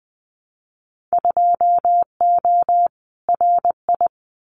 Morse code, heard as a keyed tone, about 20 wpm, 700 Hz. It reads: 2O RI
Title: Winter Field Day Exchanges 20wpm